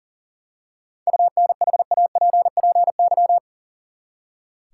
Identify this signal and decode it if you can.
Morse code, heard as a keyed tone, about 40 wpm, 700 Hz. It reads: UNHAPPY
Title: Seven Letter Words - 40wpm